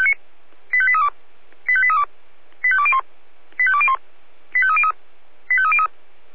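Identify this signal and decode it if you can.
Five-tone selective call (ZVEI), heard as quick runs of five tones, tones of about 70 ms. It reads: 85832 85832 85201 85201 85303 85303
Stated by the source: Five-tone ZVEI selective call received with an SDR.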